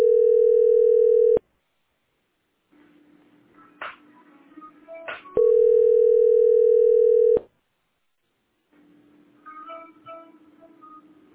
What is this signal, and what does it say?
Ringback tone (North America), 440 + 480 Hz.